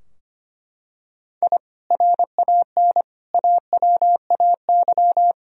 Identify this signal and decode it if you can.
Morse code, heard as a keyed tone, about 25 wpm, 700 Hz. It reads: I RAN AWAY
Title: Sentences from Top 1000 Words - Review Part 3 - 25wpm